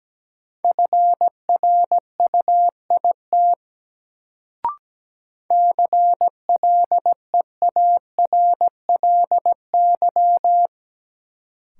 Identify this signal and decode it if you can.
Morse code, heard as a keyed tone, about 17 wpm, 700 Hz. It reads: FRUIT CLEARLY
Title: Sentences from Top 1300 Words - New Word Drill - 17wpm